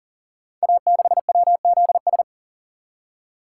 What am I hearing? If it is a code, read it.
Morse code, heard as a keyed tone, about 40 wpm, 700 Hz. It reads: A6WZS
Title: Code Groups - 5 Letter:Numbers - 40wpm